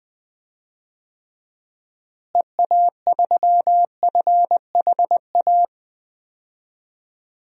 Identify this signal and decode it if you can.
Morse code, heard as a keyed tone, about 20 wpm, 700 Hz. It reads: EA3FHA
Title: DXCC-100 Call Signs - 6 Characters - Repeated 3x - 20wpm